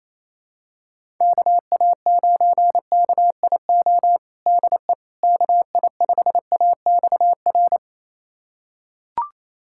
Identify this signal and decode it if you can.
Morse code, heard as a keyed tone, about 28 wpm, 700 Hz. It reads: KA9KIO DE KI5AXR
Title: USA Callsign DE Callsign - 6 Characters - 28wpm